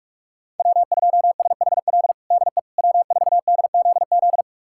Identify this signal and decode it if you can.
Morse code, heard as a keyed tone, about 46 wpm, 700 Hz. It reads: W2SHL DE W4DZZ
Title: USA Callsign DE Callsign - 5 Characters - 45wpm